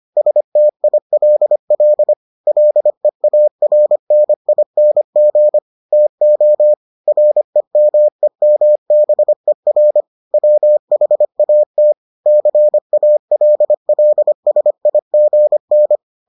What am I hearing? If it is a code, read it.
Morse code, heard as a keyed tone, about 25 wpm, 600 Hz. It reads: STILL LEARNING TO REMEMBER WHAT CALLSIGN